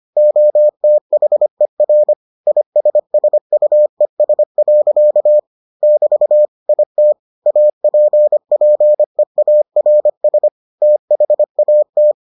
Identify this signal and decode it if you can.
Morse code, heard as a keyed tone, about 25 wpm, 600 Hz. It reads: OTHER ISSUES. <BT> IT APPEARS THAT